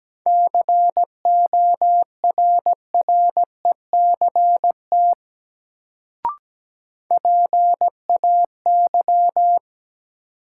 Morse code, heard as a keyed tone, about 17 wpm, 700 Hz. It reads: CORRECT PAY
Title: Sentences from Top 700 Words - New Word Drill - 17wpm